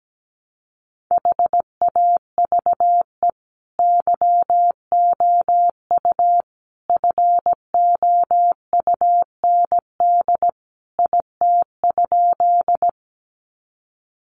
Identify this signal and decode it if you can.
Morse code, heard as a keyed tone, about 17 wpm, 700 Hz. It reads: HAVE YOU FOUND IT?
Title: Sentences from Top 300 Words 17wpm